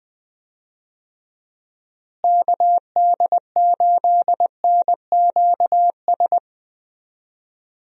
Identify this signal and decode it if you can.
Morse code, heard as a keyed tone, about 20 wpm, 700 Hz. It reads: KD8NQS
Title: USA Call Signs - 6 Characters - Repeated 3x - 20wpm